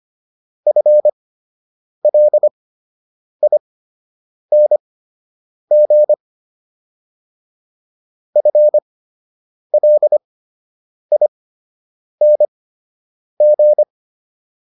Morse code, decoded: FLING FLING

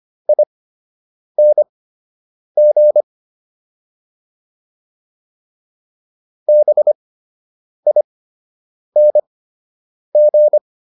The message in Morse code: ING BING

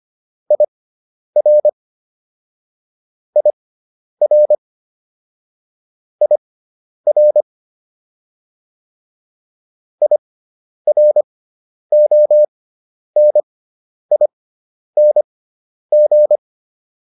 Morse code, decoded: IR IR IR IRONING

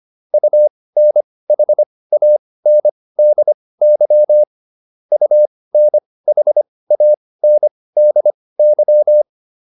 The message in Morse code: UNHANDY UNHANDY

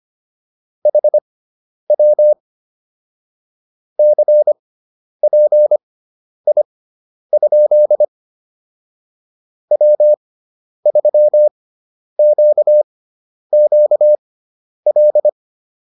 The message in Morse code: HW CPI? W3QQL

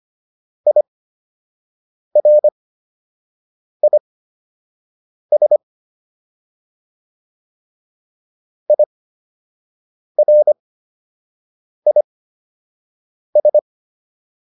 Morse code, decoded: IRIS IRIS

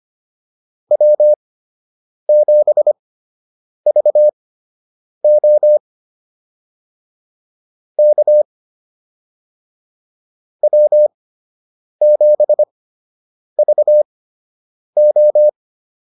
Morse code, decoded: W7VO K W7VO

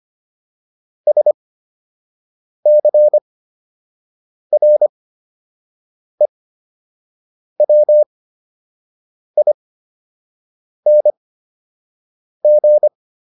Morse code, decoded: SCREWING

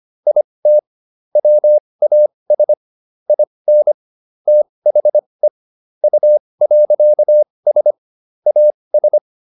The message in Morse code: IT WAS IN THE U.S AS

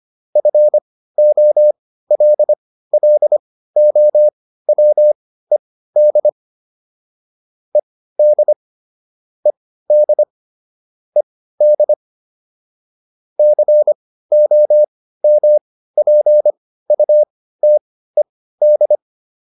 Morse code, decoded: FOLLOWED ED ED ED COMPUTED

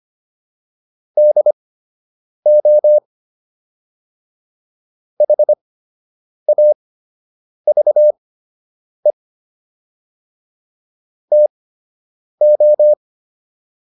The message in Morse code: DO HAVE TO

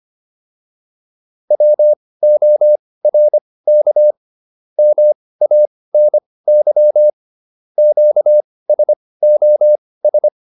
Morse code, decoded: WORK MANY QSOS

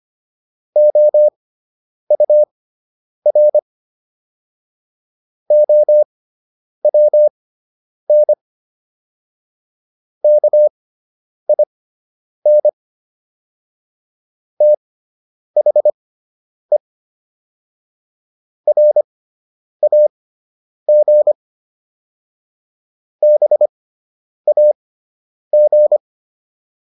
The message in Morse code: OUR OWN KIN THE RAG BAG

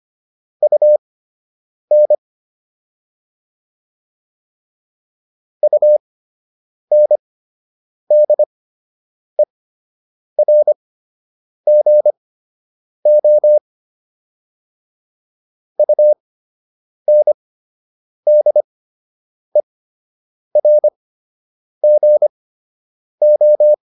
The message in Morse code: UN UNDERGO UNDERGO

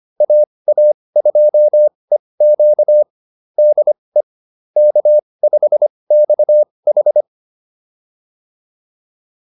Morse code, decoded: AA2EQ DE K5XH

